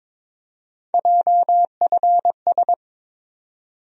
Morse code, heard as a keyed tone, about 22 wpm, 700 Hz. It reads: JFS
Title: USA Call Sign Postfixes - 3 Letters 22wpm